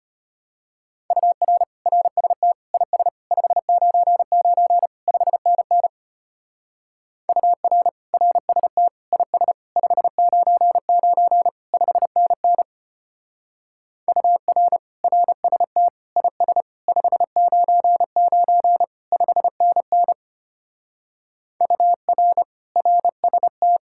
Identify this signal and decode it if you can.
Morse code, heard as a keyed tone, about 30 wpm, 700 Hz. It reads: UR RST IS 599 5NN UR RST IS 599 5NN UR RST IS 599 5NN UR RST
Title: Speed-Racing Common QSO Phrases - 25wpm